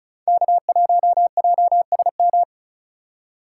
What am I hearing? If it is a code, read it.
Morse code, heard as a keyed tone, about 36 wpm, 700 Hz. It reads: K1JSM
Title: USA Call Signs - 5 Characters - 2x Character Spacing - 35wpm